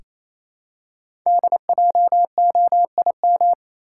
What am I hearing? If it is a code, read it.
Morse code, heard as a keyed tone, about 27 wpm, 700 Hz. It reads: DJOIM
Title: Code Groups - 5 Letters - 28wpm